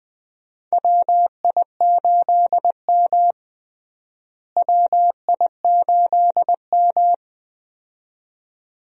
Morse code, heard as a keyed tone, about 20 wpm, 700 Hz. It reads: WI8M WI8M
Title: USA Call Signs - 4 Characters - Repeated 2x - 20wpm